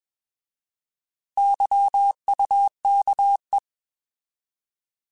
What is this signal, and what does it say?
Morse code, heard as a keyed tone, about 21 wpm, 800 Hz. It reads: YUKE